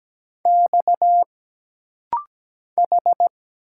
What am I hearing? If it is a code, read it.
Morse code, heard as a keyed tone, about 17 wpm, 700 Hz. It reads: X H
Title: Single Letters - Rapid-Fire 17wpm